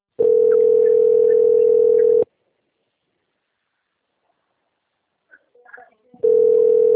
Ringback tone (North America), 440 + 480 Hz.